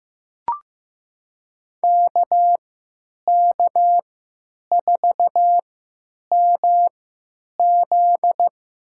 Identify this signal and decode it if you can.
Morse code, heard as a keyed tone, about 15 wpm, spaced as for 9 wpm, 700 Hz. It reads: KK4MZ
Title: USA Call Signs - 3x Character Spacing - 15wpm